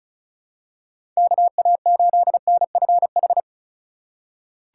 Morse code, heard as a keyed tone, about 34 wpm, 700 Hz. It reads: KA8NFH
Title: USA Call Signs - 6 Characters - 3x Character Spacing - 35wpm